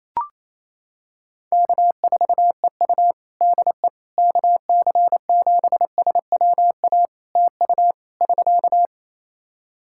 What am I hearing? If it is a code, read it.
Morse code, heard as a keyed tone, about 27 wpm, 700 Hz. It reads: K4EU DE KC7SWA TU <SK>
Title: Practice QSO - Exchange 4 - Ending - 28wpm